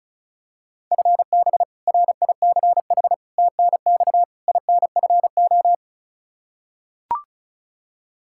Morse code, heard as a keyed tone, about 36 wpm, 700 Hz. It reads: FB RICH TNX INFO
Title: Practice QSO - Exchange 3 - More Info - 35wpm